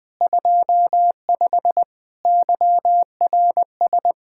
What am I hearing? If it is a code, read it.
Morse code, heard as a keyed tone, about 20 wpm, 700 Hz. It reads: 25 YRS
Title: Speed-Racing Common QSO Phrases - 17wpm